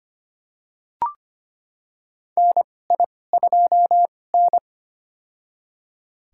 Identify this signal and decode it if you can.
Morse code, heard as a keyed tone, about 25 wpm, spaced as for 18 wpm, 700 Hz. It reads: NI2N
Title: USA Call Signs - 4 Characters - 2x Character Spacing - 25wpm